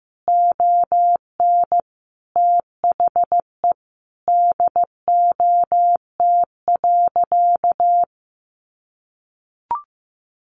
Morse code, heard as a keyed tone, about 14 wpm, 700 Hz. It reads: ON THE DOT.
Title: Sentences from Top 1500 Words 15wpm